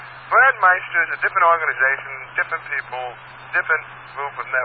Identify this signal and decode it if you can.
Emergency Alert System (SAME) tones, heard as harsh buzzing bursts, with speech. It is heard from 0.7 to 3.9 s.